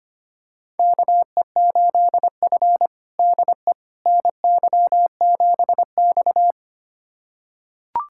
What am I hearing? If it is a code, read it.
Morse code, heard as a keyed tone, about 25 wpm, 700 Hz. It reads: KE8F DE NY7X
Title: USA Callsign DE Callsign - 4 Characters - 25wpm